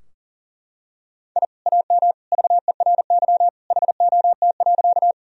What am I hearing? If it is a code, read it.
Morse code, heard as a keyed tone, about 40 wpm, 700 Hz. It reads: I AM VERY HOT.
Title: Sentences from Top 1000 Words - Review Part 3 - 40wpm